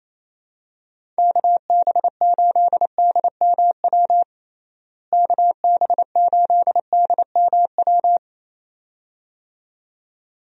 Morse code, heard as a keyed tone, about 28 wpm, 700 Hz. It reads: KB8DMW KB8DMW
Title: USA Call Signs - 6 Characters - Repeated 2x - 28wpm